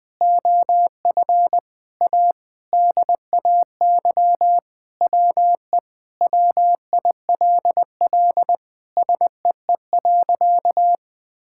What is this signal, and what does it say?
Morse code, heard as a keyed tone, about 21 wpm, 700 Hz. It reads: OF A DAY WE WILL SEE.
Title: Sentences with 3-Word Phrases - Spoken Beginning - 20wpm